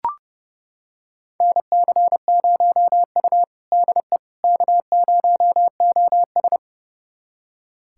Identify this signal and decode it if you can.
Morse code, heard as a keyed tone, about 30 wpm, 700 Hz. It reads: NC0U DE K0OS
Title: USA Callsign DE Callsign - 4 Characters - 30wpm